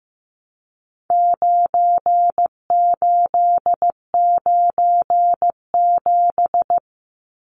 Morse code, heard as a keyed tone, about 15 wpm, 700 Hz. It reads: 9897
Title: Numbers - 4-Digit 15wpm